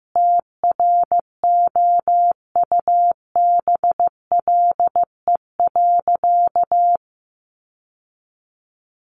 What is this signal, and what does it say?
Morse code, heard as a keyed tone, about 15 wpm, 700 Hz. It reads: TROUBLE.
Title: Sentences from Top 1200 Words 15wpm